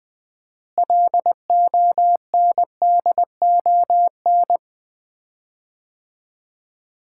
Morse code, heard as a keyed tone, about 20 wpm, 700 Hz. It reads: LONDON
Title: Common QSO Phrases - 4x Word-Spacing - 20wpm